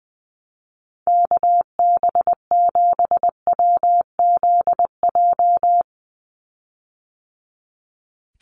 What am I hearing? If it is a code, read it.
Morse code, heard as a keyed tone, about 20 wpm, 700 Hz. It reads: KB7WZJ
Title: USA Call Signs 20wpm